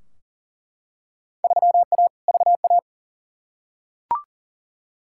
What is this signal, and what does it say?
Morse code, heard as a keyed tone, about 40 wpm, 700 Hz. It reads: 3A VA
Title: Field Day Exchanges - 40wpm